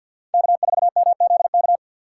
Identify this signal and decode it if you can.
Morse code, heard as a keyed tone, about 50 wpm, 700 Hz. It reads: K4MZK
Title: USA Call Signs - 6x Character Spacing - 50wpm